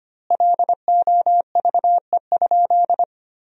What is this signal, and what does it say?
Morse code, heard as a keyed tone, about 25 wpm, 700 Hz. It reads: LOVE?
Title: 3x Word-Spacing - Sentences from Top 500 Words - 25wpm